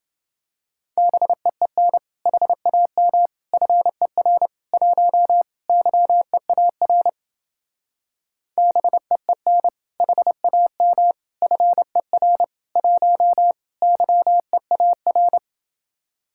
Morse code, heard as a keyed tone, about 29 wpm, 700 Hz. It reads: BEEN HAM FER 1 YEAR BEEN HAM FER 1 YEAR